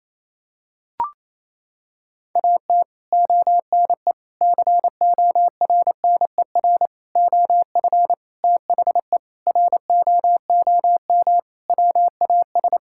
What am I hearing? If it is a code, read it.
Morse code, heard as a keyed tone, about 28 wpm, 700 Hz. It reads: AT ONE CORNER OF THE ROOM WAS
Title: Sentences from Top 1600 Words 28wpm